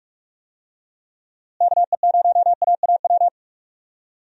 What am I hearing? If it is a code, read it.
Morse code, heard as a keyed tone, about 46 wpm, 700 Hz. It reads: KE0AAW
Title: USA Call Signs - 6 Characters - Repeated 2x - 45wpm